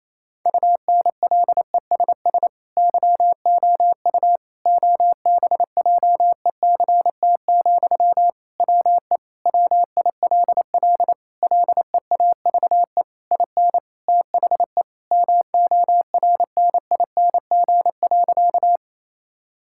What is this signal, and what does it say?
Morse code, heard as a keyed tone, about 28 wpm, 700 Hz. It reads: UNLESS YOU OBJECT, WE WILL LEAVE IN THE MORNING.
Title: Sentences from Top 1000 Words - Review Part 6 - 28wpm